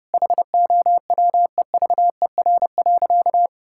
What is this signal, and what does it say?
Morse code, heard as a keyed tone, about 30 wpm, 700 Hz. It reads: HOWEVER.
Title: Sentences from Top 1000 Words - Review Part 5 - 30wpm